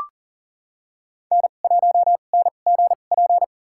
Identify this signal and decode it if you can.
Morse code, heard as a keyed tone, about 40 wpm, spaced as for 29 wpm, 700 Hz. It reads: N1NGP
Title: USA Call Signs - 2x Character Spacing - 40wpm